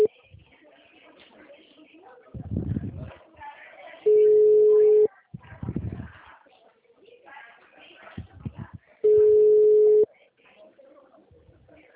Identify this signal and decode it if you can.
Ringback tone (Europe, 425 Hz), with speech.
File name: out- -2003- - - .4360